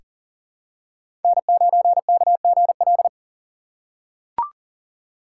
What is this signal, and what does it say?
Morse code, heard as a keyed tone, about 40 wpm, 700 Hz. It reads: N9KGL
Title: USA Call Signs - 3x Character Spacing - 40wpm